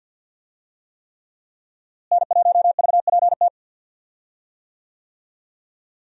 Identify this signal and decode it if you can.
Morse code, heard as a keyed tone, about 46 wpm, 700 Hz. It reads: N1VPT
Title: USA Call Signs - 5 Characters - Repeated 3x - 50wpm